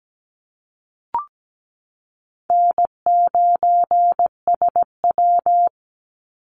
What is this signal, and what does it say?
Morse code, heard as a keyed tone, about 17 wpm, 700 Hz. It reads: N9SW
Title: Easy Call Signs Repeated 2x - 17wpm